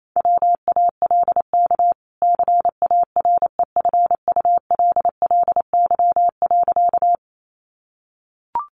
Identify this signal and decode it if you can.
Morse code, heard as a keyed tone, about 28 wpm, 700 Hz. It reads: WALK CAREFULLY.
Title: Sentences from Top 1300 Words 28wpm